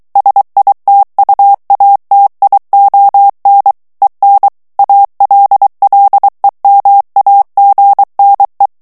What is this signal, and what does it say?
Morse code, heard as a keyed tone, about 23 wpm, 800 Hz. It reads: SITUATION EN ALLEMAGNE
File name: morse mp3 / FRANCAIS TEXTE CLAIR 1 VITESSE 22 MOTS MINUTES